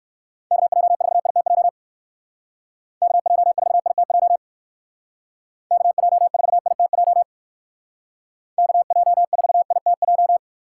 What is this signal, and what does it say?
Morse code, heard as a keyed tone, about 56 wpm, 700 Hz. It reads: KJ4ITJ KJ4ITJ KJ4ITJ KJ4ITJ
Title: Speed Racing - USA 2x3 Call Signs 45wpm